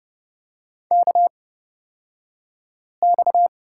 Morse code, heard as a keyed tone, about 30 wpm, spaced as for 5 wpm, 700 Hz. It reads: KX
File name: Lesson 37d - Letter x - Two Characters 30wpm